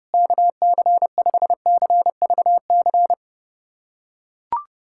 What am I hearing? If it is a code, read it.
Morse code, heard as a keyed tone, about 32 wpm, 700 Hz. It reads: KC5CVC
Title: USA Callsign DE Callsign 30wpm